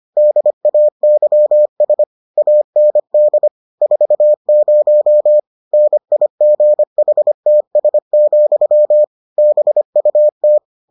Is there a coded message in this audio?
Yes: Morse code from 0.2 to 10.6 s, heard as a keyed tone, about 25 wpm, 600 Hz. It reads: DAYS AND 40 NIGHTS, BUT